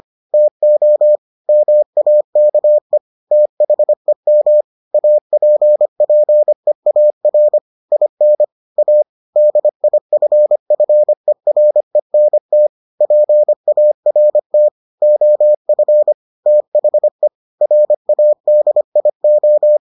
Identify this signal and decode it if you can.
Morse code, heard as a keyed tone, about 25 wpm, 600 Hz. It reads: TO MAKE THEM APPEAR IN A DIFFERENT PART OF THE RADIO